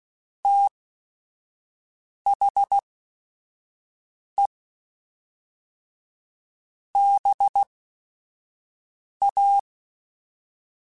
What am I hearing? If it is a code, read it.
Morse code, heard as a keyed tone, about 16 wpm, spaced as for 5 wpm, 800 Hz. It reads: THE BA
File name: Technical Subjects / Technical Subjects-mp3s / AND Gate @5